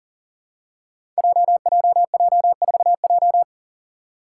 Morse code, heard as a keyed tone, about 42 wpm, 700 Hz. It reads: JJJ4J
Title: Five Challenging Letter-Number Combinations 40wpm